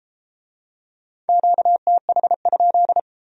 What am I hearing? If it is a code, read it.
Morse code, heard as a keyed tone, about 31 wpm, 700 Hz. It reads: QTH?